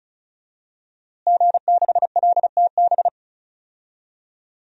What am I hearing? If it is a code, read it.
Morse code, heard as a keyed tone, about 36 wpm, 700 Hz. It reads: G6LTB
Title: DXCC-100 Call Signs - 5 Characters - Repeated 2x - 35wpm